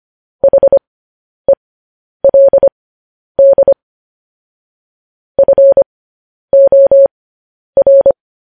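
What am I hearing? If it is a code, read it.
Morse code, heard as a keyed tone, about 25 wpm, spaced as for 10 wpm, 550 Hz. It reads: HELD FOR
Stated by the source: Morse code news headlines broadcast.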